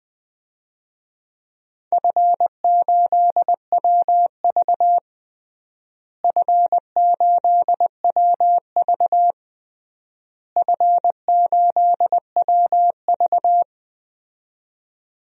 Morse code, heard as a keyed tone, about 21 wpm, 700 Hz. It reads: F8WV F8WV F8WV
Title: DXCC-100 Call Signs - 4 Characters - Repeated 3x - 20wpm